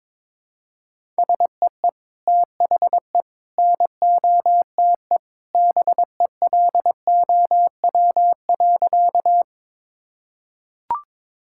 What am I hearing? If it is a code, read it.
Morse code, heard as a keyed tone, about 22 wpm, 700 Hz. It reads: SEE THE NOTE BELOW.
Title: Sentences from Top 1000 Words - Review Part 5 - 22wpm